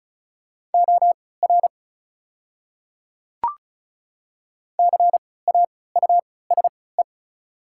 Morse code, heard as a keyed tone, about 36 wpm, spaced as for 20 wpm, 690 Hz. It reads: OR CAUSE
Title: 3x Character Spacing - Top 300 Words - 35wpm